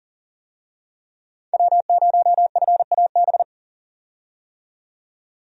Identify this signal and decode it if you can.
Morse code, heard as a keyed tone, about 40 wpm, 700 Hz. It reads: W0FAB
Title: USA Call Signs - 5 Characters - 40wpm